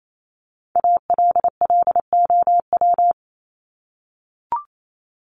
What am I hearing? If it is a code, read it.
Morse code, heard as a keyed tone, about 28 wpm, 700 Hz. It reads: ALLOW
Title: Top 501-750 Words 28wpm